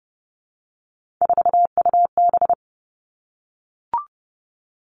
Morse code, heard as a keyed tone, about 31 wpm, 700 Hz. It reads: 4UB
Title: Three Challenging Letter-Number Combinations 30wpm